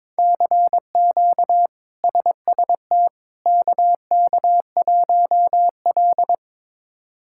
Morse code, heard as a keyed tone, about 22 wpm, 700 Hz. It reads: CQ SST KK1L
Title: K1USN SST QSO Practice - 22wpm